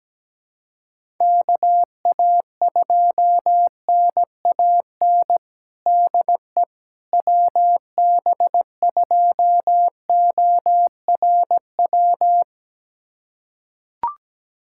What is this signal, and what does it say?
Morse code, heard as a keyed tone, about 17 wpm, 700 Hz. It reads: KA2NAN DE WB2ORW
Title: USA Callsign DE Callsign 17wpm